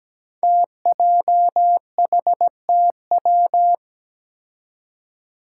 Morse code, heard as a keyed tone, about 17 wpm, 700 Hz. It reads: TJHTW
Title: Five Challenging Letter-Number Combinations 17wpm